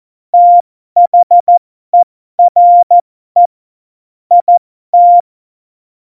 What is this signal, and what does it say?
Morse code, heard as a keyed tone, about 14 wpm, spaced as for 12 wpm, 700 Hz. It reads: THERE IT
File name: Morsekurs / Witze